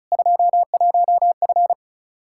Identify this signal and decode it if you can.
Morse code, heard as a keyed tone, about 36 wpm, 700 Hz. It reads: 21F
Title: Practice QSO - Exchange 2 - Info - 35wpm